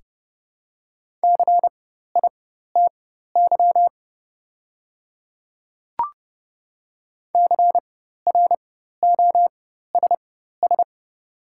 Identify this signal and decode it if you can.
Morse code, heard as a keyed tone, about 30 wpm, spaced as for 14 wpm, 700 Hz. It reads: CITY CROSS
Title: 4x Character Spacing - Top 300 Words - 30wpm